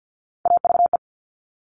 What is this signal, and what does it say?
Morse code, heard as a keyed tone, about 50 wpm, 700 Hz. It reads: AVE
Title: Top Three Letter Combinations in Words 50wpm